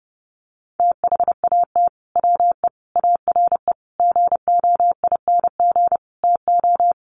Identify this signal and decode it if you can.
Morse code, heard as a keyed tone, about 30 wpm, 700 Hz. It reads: THAT WE ARE GOING TO
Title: Sets of 5 Words Encore 30wpm